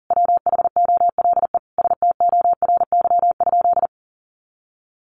Morse code, heard as a keyed tone, about 40 wpm, 700 Hz. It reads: WHOLE STORY?
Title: Sentences from Top 400 Words 40wpm